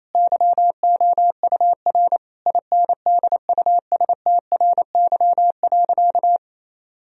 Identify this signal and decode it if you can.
Morse code, heard as a keyed tone, about 28 wpm, 700 Hz. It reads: YOUR INDUSTRY.
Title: Sentences from Top 1000 Words - Review Part 4 - 28wpm